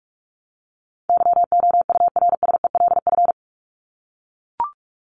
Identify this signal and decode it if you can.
Morse code, heard as a keyed tone, about 44 wpm, 700 Hz. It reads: YOURSELF
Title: Top 501-750 Words 45wpm